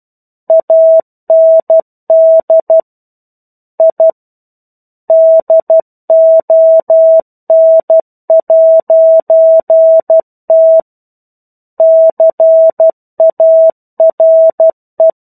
Morse code, heard as a keyed tone, about 12 wpm, 650 Hz. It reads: AND I DON'T CARE
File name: 01-Chapter-01 12 WPM